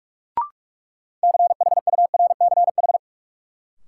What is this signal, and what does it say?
Morse code, heard as a keyed tone, about 47 wpm, 700 Hz. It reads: CHURCH
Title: Sets of 5 Words - New Word Drill - 45wpm